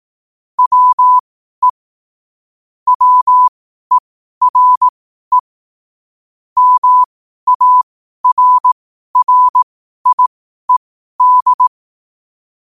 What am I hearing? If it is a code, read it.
Morse code, heard as a keyed tone, about 18 wpm, spaced as for 12 wpm, 1000 Hz. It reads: WE WERE MARRIED
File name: qotdfiles / 12wpm